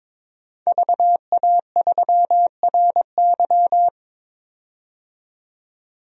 Morse code, heard as a keyed tone, about 22 wpm, 700 Hz. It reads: VA3RY
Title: DXCC-100 Call Signs - 5 Characters - 22wpm